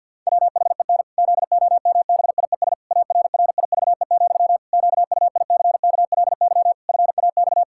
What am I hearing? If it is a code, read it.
Morse code, heard as a keyed tone, about 51 wpm, 700 Hz. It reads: WHEN ZOMBIES ARRIVE, QUICKLY FAX
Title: Pangram Challenge v2 - 50wpm